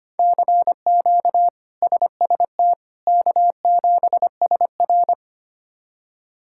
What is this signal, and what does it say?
Morse code, heard as a keyed tone, about 25 wpm, 700 Hz. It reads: CQ SST K7SR
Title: K1USN SST QSO Practice - 25wpm